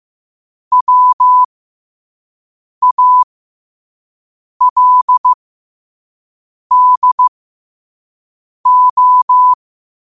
Morse code, heard as a keyed tone, about 15 wpm, spaced as for 5 wpm, 1000 Hz. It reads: WALDO
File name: qotdfiles / 5wpm